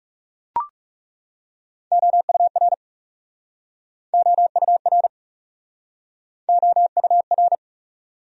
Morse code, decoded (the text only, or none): OUR OUR OUR